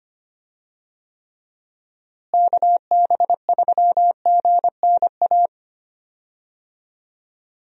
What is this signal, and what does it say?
Morse code, heard as a keyed tone, about 25 wpm, 700 Hz. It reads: KB3GNA
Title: USA Call Signs - 6 Characters - Repeated 2x - 25wpm